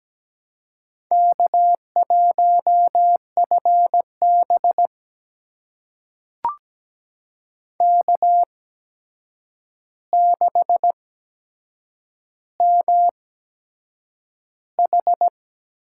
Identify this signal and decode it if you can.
Morse code, heard as a keyed tone, about 17 wpm, spaced as for 7 wpm, 700 Hz. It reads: K1FB K6MH
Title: USA Call Signs - 4 Characters - 8x Character Spacing - 17wpm